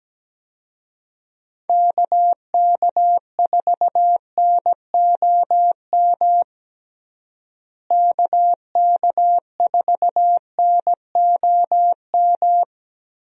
Morse code, heard as a keyed tone, about 17 wpm, 700 Hz. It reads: KK4NOM KK4NOM
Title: USA Call Signs - 6 Characters - Repeated 3x - 17wpm